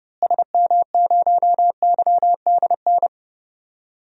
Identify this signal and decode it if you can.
Morse code, heard as a keyed tone, about 30 wpm, 700 Hz. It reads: SM0YDN
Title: DXCC-100 Call Signs - 6 Characters - Repeated 2x - 30wpm